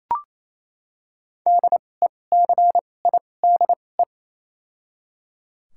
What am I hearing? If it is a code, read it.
Morse code, heard as a keyed tone, about 28 wpm, spaced as for 20 wpm, 700 Hz. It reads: DECIDE
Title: 2x Character Spacing - Top 500 Words - 28wpm